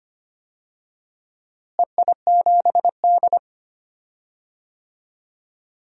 Morse code, heard as a keyed tone, about 25 wpm, 700 Hz. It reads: EI7D
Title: DXCC-100 Call Signs - 4 Characters - Repeated 2x - 25wpm